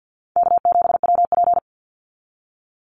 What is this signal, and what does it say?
Morse code, heard as a keyed tone, about 48 wpm, 700 Hz. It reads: K7WP
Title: Easy Call Signs Repeated 3x - 50wpm